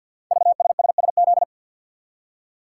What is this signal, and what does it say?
Morse code, heard as a keyed tone, about 52 wpm, 700 Hz. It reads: VSSSZ
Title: Code Groups - 5 Letters - 50wpm